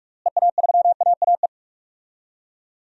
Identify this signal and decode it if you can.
Morse code, heard as a keyed tone, about 47 wpm, 700 Hz. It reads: EA3AAE
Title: DXCC-100 Call Signs - 6 Characters - Repeated 3x - 45wpm